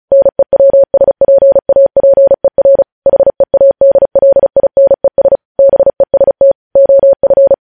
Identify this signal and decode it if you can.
Morse code, heard as a keyed tone, about 36 wpm, 550 Hz. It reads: NEWSPAPER HEADLINES BEST OF
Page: news headlines in morse code at 35 wpm